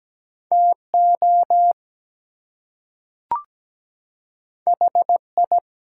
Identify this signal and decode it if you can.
Morse code, heard as a keyed tone, about 17 wpm, 700 Hz. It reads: TO HI
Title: Two Letter Words - 17wpm